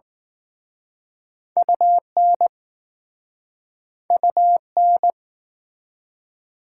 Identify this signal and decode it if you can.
Morse code, heard as a keyed tone, about 20 wpm, 700 Hz. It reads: UN UN